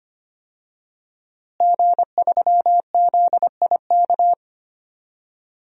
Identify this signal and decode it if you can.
Morse code, heard as a keyed tone, about 25 wpm, 700 Hz. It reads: G3ZIK
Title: DXCC-100 Call Signs - 5 Characters - Repeated 2x - 25wpm